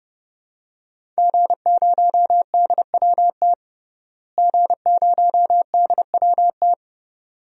Morse code, heard as a keyed tone, about 32 wpm, 700 Hz. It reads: G0DWT G0DWT
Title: DXCC-100 Call Signs - 5 Characters - Repeated 2x - 30wpm